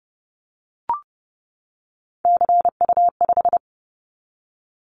Morse code, heard as a keyed tone, about 30 wpm, 700 Hz. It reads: CU5
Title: Callsign Prefixes for Top 50 DXCC - 30wpm